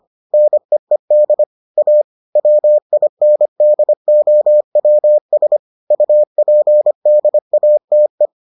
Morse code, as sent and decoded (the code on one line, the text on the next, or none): -. . -.. / .- / .-- .. -. -.. --- .-- ... / ..- .--. -.. .- - .
NEED A WINDOWS UPDATE